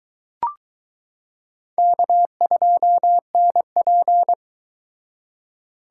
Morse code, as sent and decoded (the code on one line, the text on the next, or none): -.- ..--- -. .--.
K2NP